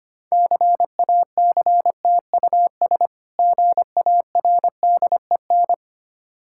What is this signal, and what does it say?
Morse code, heard as a keyed tone, about 25 wpm, 690 Hz. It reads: CACTUS GARDEN
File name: practice / 25wpm / Difficult Words with Hints - 25wpm